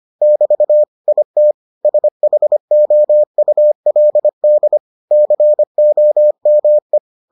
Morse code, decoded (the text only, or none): <BT> IT SHOULD COME